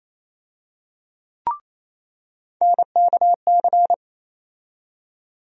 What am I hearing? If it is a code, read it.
Morse code, heard as a keyed tone, about 27 wpm, 700 Hz. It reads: NKC
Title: USA Call Sign Postfixes - 3 Letters 28wpm